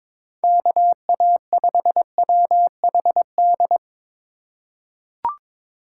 Morse code, heard as a keyed tone, about 21 wpm, 700 Hz. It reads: KA5WHD
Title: USA Call Signs - 6 Characters - 2x Character Spacing - 22wpm